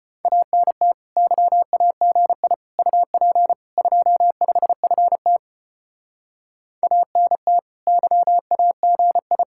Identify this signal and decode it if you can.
Morse code, heard as a keyed tone, about 32 wpm, 700 Hz. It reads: ANT YAGI UP 25FT ANT YAGI